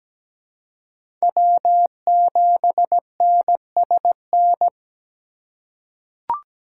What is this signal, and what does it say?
Morse code, heard as a keyed tone, about 17 wpm, 700 Hz. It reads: W7NSN
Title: USA Call Signs - 5 Characters - 2x Character Spacing - 17wpm